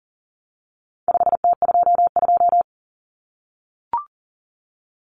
Morse code, heard as a keyed tone, about 40 wpm, 700 Hz. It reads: 5T22
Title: Common Cut Numbers - 4-Digit 40wpm